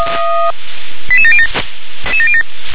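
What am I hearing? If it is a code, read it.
Five-tone selective call (ZVEI), heard as quick runs of five tones, tones of about 70 ms. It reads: 88707
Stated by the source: ZVEI five-tone call received by SDR.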